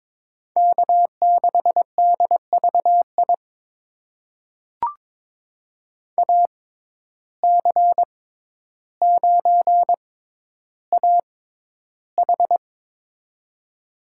Morse code, heard as a keyed tone, about 22 wpm, spaced as for 11 wpm, 700 Hz. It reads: K6DVI AC9AH
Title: USA Call Signs - 5 Characters - 6x Character Spacing - 22wpm